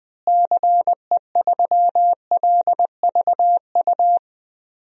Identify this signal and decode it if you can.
Morse code, heard as a keyed tone, about 20 wpm, 700 Hz. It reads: CE3LVU
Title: DXCC-100 Call Signs - 6 Characters - Repeated 2x - 20wpm